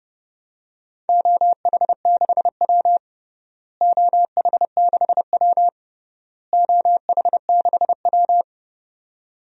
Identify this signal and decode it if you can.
Morse code, heard as a keyed tone, about 32 wpm, 700 Hz. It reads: OH6W OH6W OH6W
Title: DXCC-100 Call Signs - 4 Characters - Repeated 3x - 30wpm